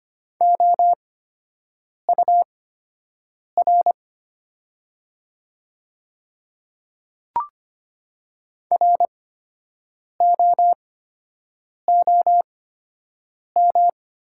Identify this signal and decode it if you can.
Morse code, heard as a keyed tone, about 25 wpm, spaced as for 7 wpm, 700 Hz. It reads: OUR ROOM
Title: 8x Character Spacing - Top 300 Words - 25wpm